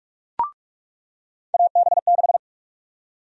Morse code, heard as a keyed tone, about 47 wpm, 700 Hz. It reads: AB6
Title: USA Call Sign Prefixes - 45wpm